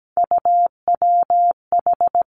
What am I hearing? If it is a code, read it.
Morse code, heard as a keyed tone, about 17 wpm, 700 Hz. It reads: UWH
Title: Three Challenging Letter-Number Combinations 17wpm